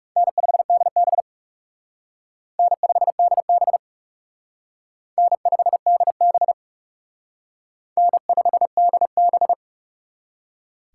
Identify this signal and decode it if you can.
Morse code, heard as a keyed tone, about 35 wpm, 700 Hz. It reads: N5DB N5DB N5DB N5DB
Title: Speed Racing - USA 1x2 Call Signs 30wpm